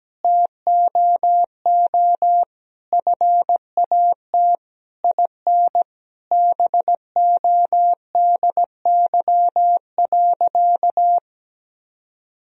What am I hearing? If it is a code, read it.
Morse code, heard as a keyed tone, about 17 wpm, 700 Hz. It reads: TOO FAT IN BODY.
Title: Sentences from Top 1000 Words - Review Part 4 - 17wpm